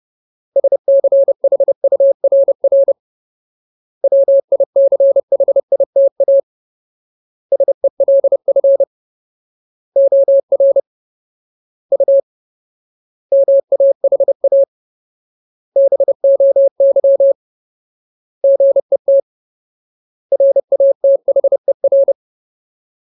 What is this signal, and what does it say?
Morse code, heard as a keyed tone, about 30 wpm, 550 Hz. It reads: SCHURR WICHITA SELF OR U MAHA BOY GET RATHER
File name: Ultimate Word List / 30WPM Ultimate